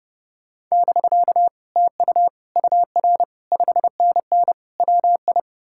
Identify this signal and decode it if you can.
Morse code, heard as a keyed tone, about 40 wpm, spaced as for 35 wpm, 700 Hz. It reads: <BK> TU UR 5NN WI
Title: Speed-Racing Common QSO Phrases - 30wpm